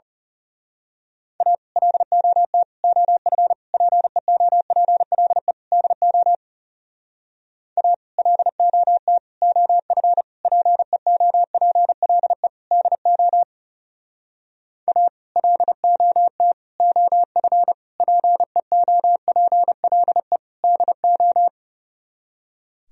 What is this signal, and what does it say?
Morse code, heard as a keyed tone, about 34 wpm, 700 Hz. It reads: A LOT OF PEOPLE DO A LOT OF PEOPLE DO A LOT OF PEOPLE DO
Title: Speed-Racing - Sets of 5 Words 30wpm